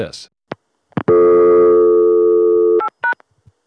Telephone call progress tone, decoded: dial tone